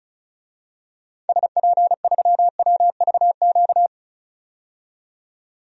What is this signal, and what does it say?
Morse code, heard as a keyed tone, about 36 wpm, 700 Hz. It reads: SP3WVQ
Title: DXCC-100 Call Signs - 6 Characters - 35wpm